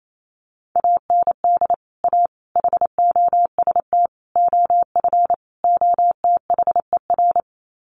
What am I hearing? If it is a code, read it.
Morse code, heard as a keyed tone, about 28 wpm, 700 Hz. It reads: AND A HOST OF OTHER
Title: Sets of 5 Words Encore2 28wpm